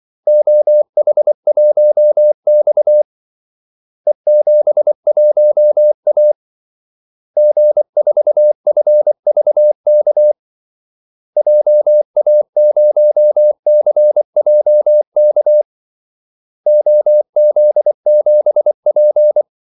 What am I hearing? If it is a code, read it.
Morse code, heard as a keyed tone, about 24 wpm, 600 Hz. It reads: OH1X E71A G4FVK JA0CJK OZ7P
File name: Callsigns-24wpm